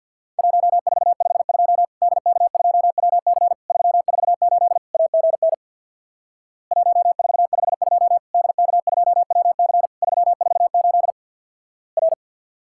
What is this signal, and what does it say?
Morse code, heard as a keyed tone, about 50 wpm, 700 Hz. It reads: 1452 DK2WC 348 AGN 1452 DK2WC 348 R
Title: Worked All Europe DX Contest - QTC Exchange with Repeats - 50wpm